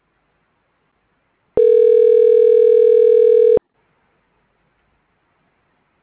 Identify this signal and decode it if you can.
Ringback tone (North America), 440 + 480 Hz.
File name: recordings / REdae9ecfb965b8b4ae a